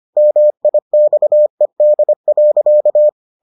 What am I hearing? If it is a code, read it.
Morse code, heard as a keyed tone, about 25 wpm, 600 Hz. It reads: MIXED.